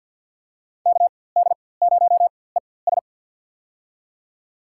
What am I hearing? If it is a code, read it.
Morse code, heard as a keyed tone, about 53 wpm, spaced as for 24 wpm, 700 Hz. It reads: KD0ES
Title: USA Call Signs - 5 Characters - 4x Character Spacing - 50wpm